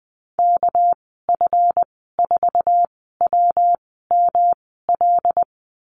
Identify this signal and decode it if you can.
Morse code, heard as a keyed tone, about 20 wpm, spaced as for 14 wpm, 700 Hz. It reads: KF4WML
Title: 219 Lesson 35g - Number 0 - Call Signs 20wpm